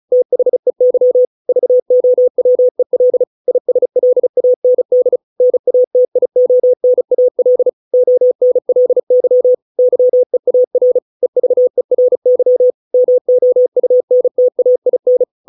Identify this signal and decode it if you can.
Morse code, heard as a keyed tone, about 34 wpm, 500 Hz. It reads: THEY VOWEL ISLAND NATIONAL ONLY YEAR EVERY MOUNTAIN